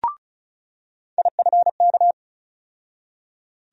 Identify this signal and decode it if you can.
Morse code, heard as a keyed tone, about 36 wpm, 700 Hz. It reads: IFK
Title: USA Call Sign Postfixes - 3 Letters 35wpm